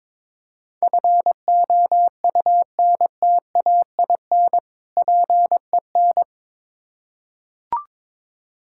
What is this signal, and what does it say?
Morse code, heard as a keyed tone, about 21 wpm, 700 Hz. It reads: FOUNTAIN PEN
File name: Difficult Words with Hints - 22wpm